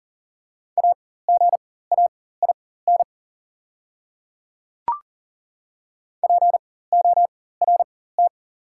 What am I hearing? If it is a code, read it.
Morse code, heard as a keyed tone, about 40 wpm, spaced as for 19 wpm, 690 Hz. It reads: AGAIN PORT
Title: 4x Character Spacing - Top 300 Words - 40wpm